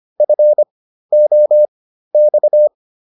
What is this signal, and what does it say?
Morse code, heard as a keyed tone, about 25 wpm, spaced as for 13 wpm, 600 Hz. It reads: FOX
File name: PR404 13